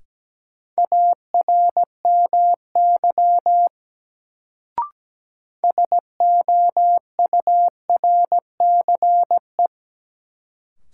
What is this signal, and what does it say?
Morse code, heard as a keyed tone, about 17 wpm, 700 Hz. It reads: ARMY SOURCE
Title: Sentences from Top 800 Words - New Word Drill - 17wpm